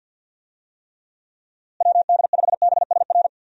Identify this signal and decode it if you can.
Morse code, heard as a keyed tone, about 45 wpm, 700 Hz. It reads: WD5BSR
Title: USA Call Signs - 6 Characters - Repeated 3x - 50wpm